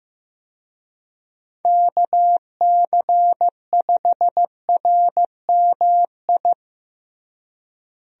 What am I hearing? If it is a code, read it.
Morse code, heard as a keyed tone, about 15 wpm, 700 Hz. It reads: KC5RMI
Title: USA Call Signs - 6 Characters - Repeated 3x - 15wpm